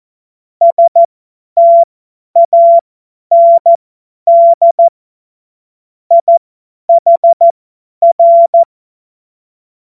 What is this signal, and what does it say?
Morse code, heard as a keyed tone, about 14 wpm, spaced as for 10 wpm, 680 Hz. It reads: STAND IHR
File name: Morsekurs / Witze / Witz 08. Februar 2022 50 F70